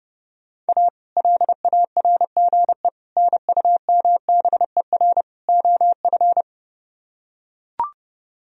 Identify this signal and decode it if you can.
Morse code, heard as a keyed tone, about 30 wpm, 700 Hz. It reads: A LARGE NUMBER OF